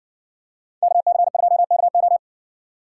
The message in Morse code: KC2KO

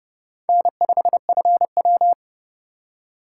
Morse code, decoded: N5FW

